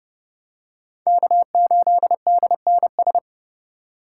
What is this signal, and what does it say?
Morse code, heard as a keyed tone, about 29 wpm, 700 Hz. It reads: K8DNS